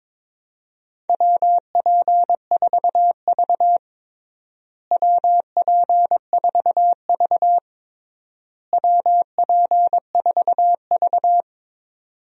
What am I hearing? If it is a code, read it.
Morse code, heard as a keyed tone, about 22 wpm, 700 Hz. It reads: WP4V WP4V WP4V
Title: USA Call Signs - 4 Characters - Repeated 3x - 22wpm